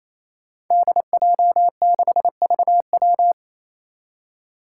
Morse code, heard as a keyed tone, about 28 wpm, 700 Hz. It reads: DJ6VW
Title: Five Challenging Letter-Number Combinations 28wpm